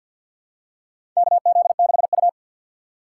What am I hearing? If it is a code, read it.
Morse code, heard as a keyed tone, about 51 wpm, 700 Hz. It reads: KZ6U
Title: USA Call Signs - 4 Characters - 3x Character Spacing - 50wpm